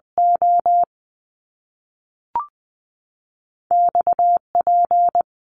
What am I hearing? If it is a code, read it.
Morse code, heard as a keyed tone, about 20 wpm, 700 Hz. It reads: O X P
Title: Top Two Letter Combinations in Words 20wpm